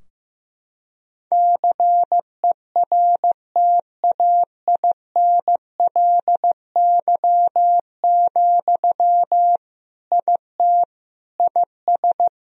Morse code, heard as a keyed tone, about 16 wpm, 700 Hz. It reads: CERTAINLY, IT IS
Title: Sentences from Top 1000 Words - Review Part 2 - 15wpm